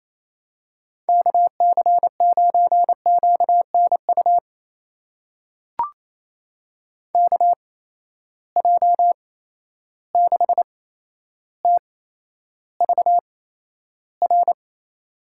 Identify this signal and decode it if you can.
Morse code, heard as a keyed tone, about 27 wpm, spaced as for 12 wpm, 700 Hz. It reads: KC9QNU KJ6TVR
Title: USA Call Signs - 6 Characters - 8x Character Spacing - 28wpm